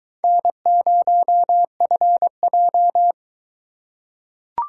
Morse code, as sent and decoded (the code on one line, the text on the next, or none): -. ----- ..-. .---
N0FJ